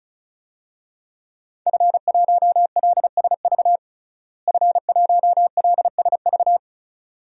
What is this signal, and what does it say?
Morse code, heard as a keyed tone, about 36 wpm, 700 Hz. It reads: F1LSV F1LSV
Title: DXCC-100 Call Signs - 5 Characters - Repeated 2x - 35wpm